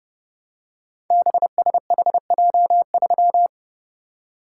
Morse code, heard as a keyed tone, about 30 wpm, 700 Hz. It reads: BSHJ3
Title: Code Groups - 5 Letter:Numbers - 30wpm